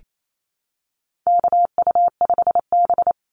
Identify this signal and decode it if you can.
Morse code, heard as a keyed tone, about 28 wpm, 700 Hz. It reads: KU5B
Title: Call Signs Repeated 2x - 28wpm